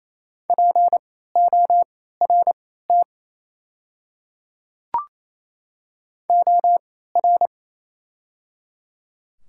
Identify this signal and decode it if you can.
Morse code, heard as a keyed tone, about 28 wpm, spaced as for 16 wpm, 700 Hz. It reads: PORT OR
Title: 3x Character Spacing - Top 200 Words - 28wpm